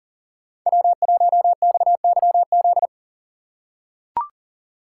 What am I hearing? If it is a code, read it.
Morse code, heard as a keyed tone, about 40 wpm, 700 Hz. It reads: W1XYZ
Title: USA Call Signs - 5 Characters - 2x Character Spacing - 40wpm